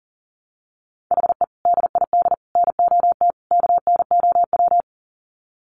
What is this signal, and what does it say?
Morse code, heard as a keyed tone, about 40 wpm, 700 Hz. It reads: HE DID NOT KNOW
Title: Sets of 4 Words 40wpm